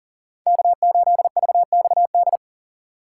Morse code, heard as a keyed tone, about 40 wpm, 700 Hz. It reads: K8VXD
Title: USA Call Signs - 5 Characters - 2x Character Spacing - 40wpm